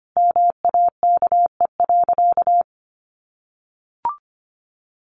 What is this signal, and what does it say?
Morse code, heard as a keyed tone, about 25 wpm, 700 Hz. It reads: MAKE.
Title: Sentences from Top 1100 Words 25wpm